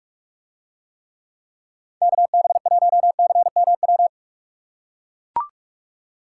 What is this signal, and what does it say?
Morse code, heard as a keyed tone, about 45 wpm, 690 Hz. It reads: KB1CMW